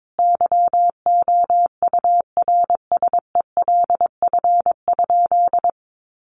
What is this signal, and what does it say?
Morse code, heard as a keyed tone, about 22 wpm, 700 Hz. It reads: YOURSELF?
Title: Sentences from Top 1100 Words 22wpm